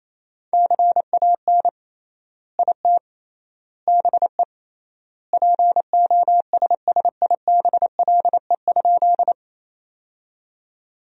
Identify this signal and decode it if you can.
Morse code, heard as a keyed tone, about 28 wpm, 700 Hz. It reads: CAN IT BE POSSIBLE?
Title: 3x Word-Spacing - Sentences from Top 500 Words - 28wpm